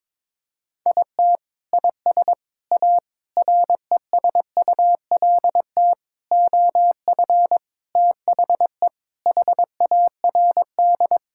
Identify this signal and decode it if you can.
Morse code, heard as a keyed tone, about 22 wpm, 700 Hz. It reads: IT IS A RESULT OF THE HARD
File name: practice / 22wpm / Sentences with 4-Word Phrases - 22wpm